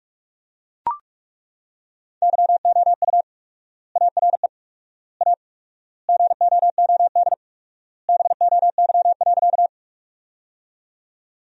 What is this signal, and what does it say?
Morse code, heard as a keyed tone, about 42 wpm, 700 Hz. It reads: YOU ARE A GOOD BOY.
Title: 4x Word-Spacing - Sentences from Top 200 Words - 45wpm